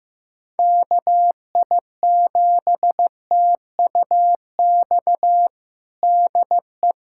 Morse code, heard as a keyed tone, about 15 wpm, 700 Hz. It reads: KI7TUX DE
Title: USA Callsign DE Callsign 15wpm